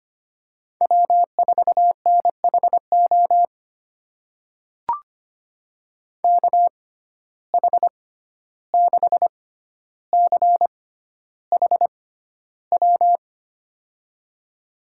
Morse code, heard as a keyed tone, about 25 wpm, spaced as for 12 wpm, 700 Hz. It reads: W4NHO KH6CHW
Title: USA Call Signs - 6x Character Spacing - 25wpm